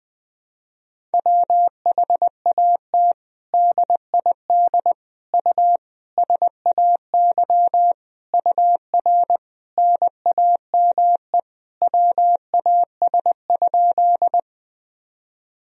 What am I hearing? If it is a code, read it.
Morse code, heard as a keyed tone, about 21 wpm, 700 Hz. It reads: WHAT DID U SAY UR NAME WAS?
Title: Sentences from Top 200 Words and Abbreviations 20wpm